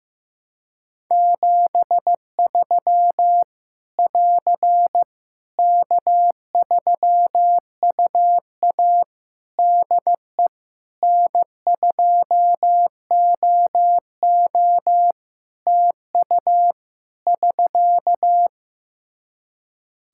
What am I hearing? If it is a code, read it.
Morse code, heard as a keyed tone, about 15 wpm, 700 Hz. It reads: 73 <AR> K3UA DE N2OO TU <SK>
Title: K1USN SST QSO Practice 15wpm